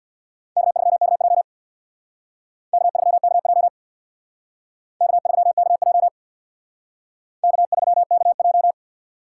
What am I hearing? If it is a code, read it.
Morse code, heard as a keyed tone, about 65 wpm, 700 Hz. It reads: K3KJ K3KJ K3KJ K3KJ